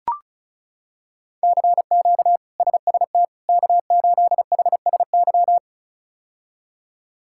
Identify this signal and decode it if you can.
Morse code, heard as a keyed tone, about 34 wpm, 700 Hz. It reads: CQ SST K8HSY
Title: K1USN SST QSO Practice - 35wpm